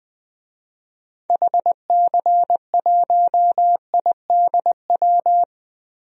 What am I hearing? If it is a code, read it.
Morse code, heard as a keyed tone, about 20 wpm, 700 Hz. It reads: HC1IDW
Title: DXCC-100 Call Signs - 6 Characters - Repeated 3x - 20wpm